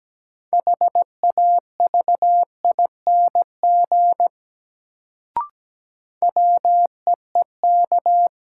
Morse code, heard as a keyed tone, about 17 wpm, 700 Hz. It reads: HAVING WEEK
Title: Sets of 5 Words Encore - New Word Drill - 17wpm